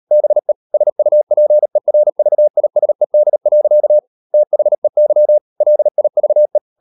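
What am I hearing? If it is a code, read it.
Morse code, heard as a keyed tone, about 38 wpm, 600 Hz. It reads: BE SUPERVISED. THEY LIVE
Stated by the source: Morse code practice at 38 WPM, sending Ken Follett's novel Never in English, from CW med Gnister.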